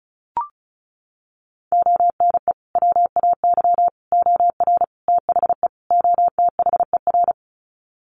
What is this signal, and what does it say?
Morse code, heard as a keyed tone, about 34 wpm, 700 Hz. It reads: ONE WAY OR THE OTHER
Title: Sets of 5 Words 35wpm